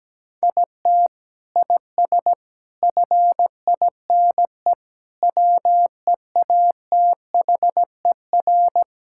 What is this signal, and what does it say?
Morse code, heard as a keyed tone, about 17 wpm, 700 Hz. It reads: IT IS FINE WEATHER
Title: Sentences from Top 1000 Words - Review Part 5 - 17wpm